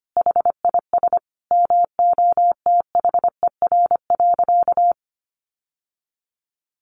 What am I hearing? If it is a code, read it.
Morse code, heard as a keyed tone, about 25 wpm, 700 Hz. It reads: HIS MOTHER.
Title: Sentences from Top 700 Words 25wpm